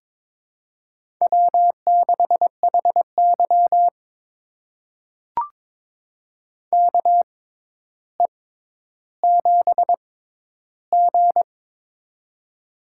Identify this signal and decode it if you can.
Morse code, heard as a keyed tone, about 22 wpm, spaced as for 11 wpm, 700 Hz. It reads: W6HY KE7G
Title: USA Call Signs - 4 Characters - 6x Character Spacing - 22wpm